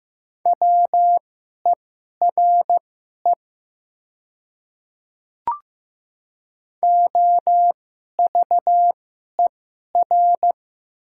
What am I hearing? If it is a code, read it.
Morse code, heard as a keyed tone, about 15 wpm, spaced as for 11 wpm, 700 Hz. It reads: WERE OVER
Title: 2x Character Spacing - Top 100 Words - 15wpm